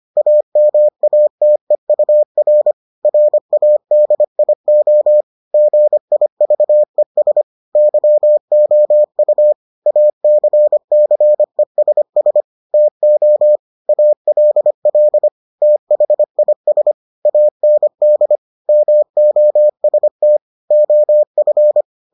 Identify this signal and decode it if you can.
Morse code, heard as a keyed tone, about 25 wpm, 600 Hz. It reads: AMATEUR RADIO GIVES YOU ACCESS TO ALL THIS AND MOST OF